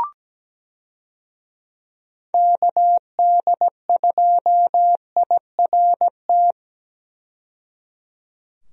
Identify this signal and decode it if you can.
Morse code, heard as a keyed tone, about 17 wpm, 700 Hz. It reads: KD2IRT
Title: USA Call Signs - 6 Characters - 17wpm